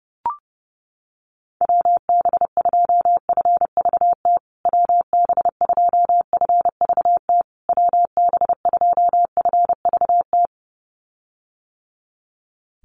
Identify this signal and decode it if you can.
Morse code, heard as a keyed tone, about 30 wpm, 700 Hz. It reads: WB2FVT WB2FVT WB2FVT
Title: USA Call Signs Repeated 3x 30wpm